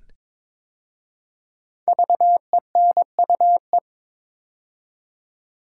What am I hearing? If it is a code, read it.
Morse code, heard as a keyed tone, about 22 wpm, 700 Hz. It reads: VENUE
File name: 5 Letter Words - First 3 Letters Given - 22wpm